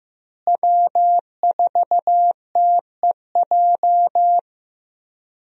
Morse code, heard as a keyed tone, about 15 wpm, 700 Hz. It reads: W4TEJ
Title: USA Call Signs - 5 Characters - Repeated 3x - 15wpm